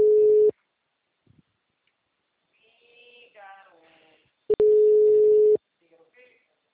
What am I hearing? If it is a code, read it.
Ringback tone (Europe, 425 Hz), with speech.